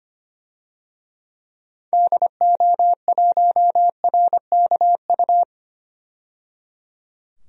Morse code, heard as a keyed tone, about 25 wpm, 700 Hz. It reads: DO1RKU
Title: DXCC-100 Call Signs - 6 Characters - 25wpm